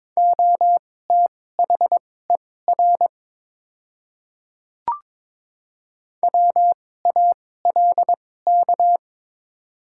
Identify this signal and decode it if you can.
Morse code, heard as a keyed tone, about 22 wpm, spaced as for 16 wpm, 700 Hz. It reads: OTHER WALK
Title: 2x Character Spacing - Top 500 Words - 22wpm